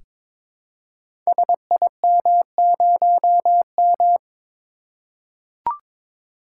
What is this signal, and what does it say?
Morse code, heard as a keyed tone, about 22 wpm, 700 Hz. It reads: SIM0M
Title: Code Groups - 5 Letter:Numbers - 22wpm